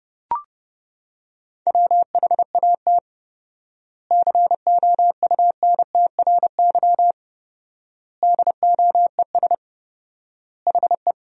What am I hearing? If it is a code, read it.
Morse code, heard as a keyed tone, about 32 wpm, 700 Hz. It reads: WHAT COUNTRY DOES HE
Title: 4x Word-Spacing - Sentences from Top 300 Words - 30wpm